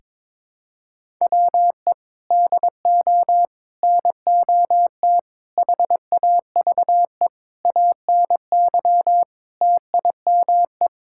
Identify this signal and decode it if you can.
Morse code, heard as a keyed tone, about 22 wpm, 700 Hz. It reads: WE DO NOT HAVE ANY TIME
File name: practice / 22wpm / Sentences with 4-Word Phrases - 22wpm